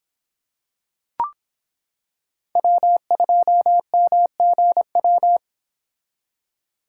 Morse code, heard as a keyed tone, about 25 wpm, 700 Hz. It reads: W2MGW